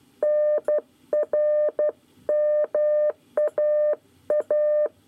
Morse code, heard as a keyed tone, about 12 wpm, 600 Hz. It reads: NRMAA
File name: 9 lezione cw 1